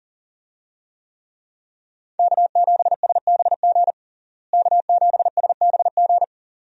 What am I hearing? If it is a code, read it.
Morse code, heard as a keyed tone, about 37 wpm, 700 Hz. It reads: K7SBG K7SBG
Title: USA Call Signs - 5 Characters - Repeated 2x - 40wpm